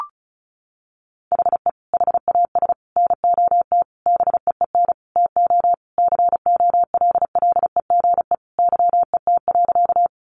Morse code, heard as a keyed tone, about 34 wpm, 700 Hz. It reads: HE HAS NOT BEEN TO COLLEGE YET.
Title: Sentences from Top 1300 Words 35wpm